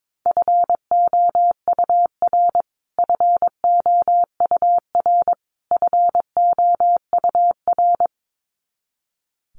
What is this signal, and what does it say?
Morse code, heard as a keyed tone, about 22 wpm, 700 Hz. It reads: FOUR FOUR FOUR
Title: Top 500 Words Repeated 22wpm